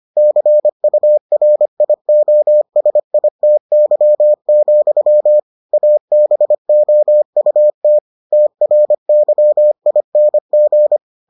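Morse code, heard as a keyed tone, about 25 wpm, 600 Hz. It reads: CURIOSITY, ABOUT TRYING
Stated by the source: Morse edition of the Foundations of Amateur Radio episodes.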